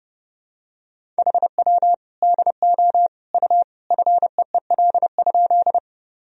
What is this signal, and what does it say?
Morse code, heard as a keyed tone, about 28 wpm, 700 Hz. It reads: HW DO U FEEL?
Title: Sentences from Top 400 Words and Abbreviations 30wpm